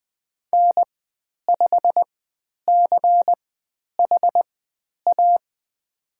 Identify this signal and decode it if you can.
Morse code, heard as a keyed tone, about 20 wpm, spaced as for 10 wpm, 700 Hz. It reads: N5CHA
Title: 095 Lesson 16g - Letter c - Call Signs 20wpm